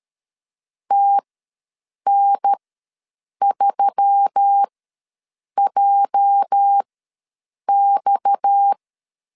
Morse code, heard as a keyed tone, about 13 wpm, spaced as for 7 wpm, 800 Hz. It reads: TN3JX
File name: LEZIONI / 9 286 ws 12wpm 3351